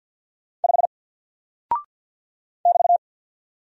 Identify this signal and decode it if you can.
Morse code, heard as a keyed tone, about 48 wpm, 700 Hz. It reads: 5 <BT>
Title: Single Character (Letter:Number:Punctuation:Prosign) Alt 50wpm